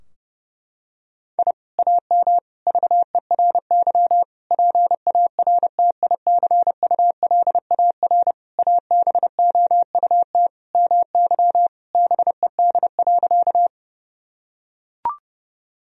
Morse code, heard as a keyed tone, about 32 wpm, 700 Hz. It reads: I AM VERY PARTICULAR ABOUT MY BED.
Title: Sentences from Top 1000 Words - Review Part 4 - 30wpm